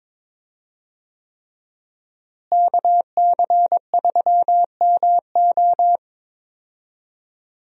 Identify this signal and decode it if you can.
Morse code, heard as a keyed tone, about 22 wpm, 700 Hz. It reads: KC3MO
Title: USA Call Signs - 5 Characters - Repeated 3x - 22wpm